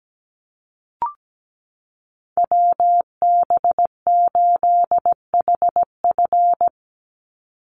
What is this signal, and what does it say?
Morse code, heard as a keyed tone, about 17 wpm, 700 Hz. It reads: WB8HF
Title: Call Signs Repeated 3x - 17wpm